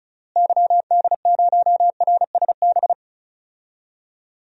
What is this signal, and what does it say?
Morse code, heard as a keyed tone, about 36 wpm, 700 Hz. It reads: YD0RSB
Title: DXCC-100 Call Signs - 6 Characters - Repeated 3x - 35wpm